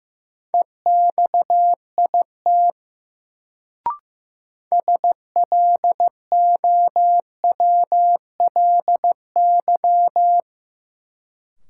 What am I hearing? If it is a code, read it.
Morse code, heard as a keyed tone, about 15 wpm, 700 Hz. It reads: EXIT SLOWLY
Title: Sentences from Top 1100 Words - New Word Drill - 15wpm